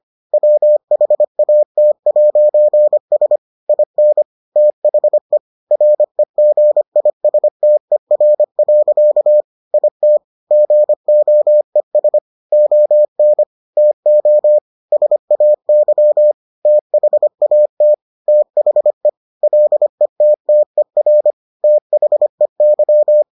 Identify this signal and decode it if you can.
Morse code, heard as a keyed tone, about 25 wpm, 600 Hz. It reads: WHAT'S IN THE REGISTER. IT GOES ON TO SAY THAT THE LETTER THEY